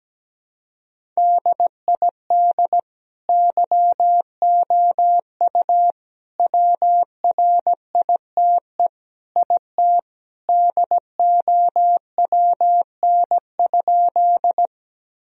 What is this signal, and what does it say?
Morse code, heard as a keyed tone, about 17 wpm, 700 Hz. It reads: DID YOU WRITE IT DOWN?
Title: Sentences from Top 1000 Words - Review Part 1 - 17wpm